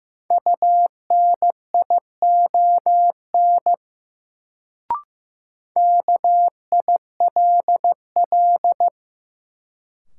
Morse code, heard as a keyed tone, about 16 wpm, 700 Hz. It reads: UNION KILL
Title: Sentences from Top 1400 Words - New Word Drill Alt - 15wpm